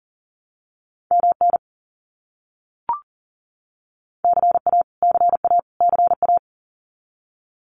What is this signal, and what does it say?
Morse code, heard as a keyed tone, about 40 wpm, 700 Hz. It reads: MN CA CA CA
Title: US States Repeated 3x 40wpm